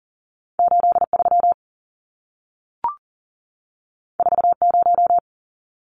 Morse code, heard as a keyed tone, about 40 wpm, 700 Hz. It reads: 83 40
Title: Numbers - 2-Digit 40wpm